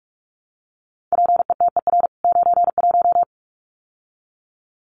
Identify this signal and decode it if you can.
Morse code, heard as a keyed tone, about 44 wpm, 700 Hz. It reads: PETER 91
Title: CWT Exchange 45wpm